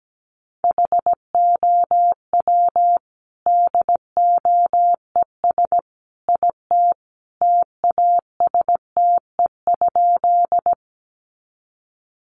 Morse code, heard as a keyed tone, about 17 wpm, 700 Hz. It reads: HOW DOES IT TASTE?
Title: Sentences from Top 1100 Words 17wpm